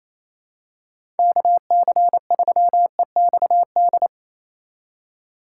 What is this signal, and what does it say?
Morse code, heard as a keyed tone, about 27 wpm, 700 Hz. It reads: KC3EXD
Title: USA Call Signs - 6 Characters - 3x Character Spacing - 28wpm